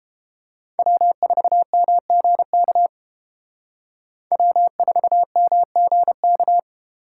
Morse code, heard as a keyed tone, about 29 wpm, 700 Hz. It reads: W4MGK W4MGK